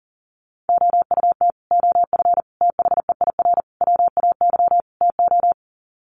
Morse code, heard as a keyed tone, about 40 wpm, 700 Hz. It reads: OUT OF THEIR WAY TO
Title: Sets of 5 Words Encore2 40wpm